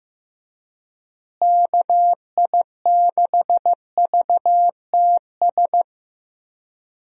Morse code, heard as a keyed tone, about 15 wpm, 700 Hz. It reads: KI6VTS
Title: USA Call Signs - 6 Characters - Repeated 2x - 15wpm